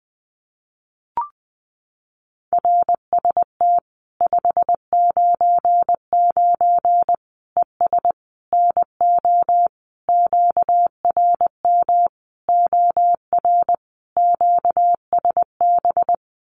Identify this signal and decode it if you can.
Morse code, heard as a keyed tone, about 20 wpm, 700 Hz. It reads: RST 599 ES NO QRM OR QSB
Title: Common QSO Phrases 20wpm